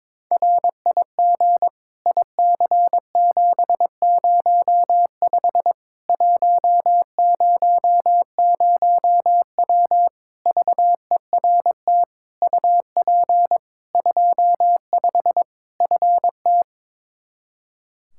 Morse code, heard as a keyed tone, about 22 wpm, 700 Hz. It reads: RIG IC705 100W VERT UP 25 FT
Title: K1USN SST QSO Practice 22wpm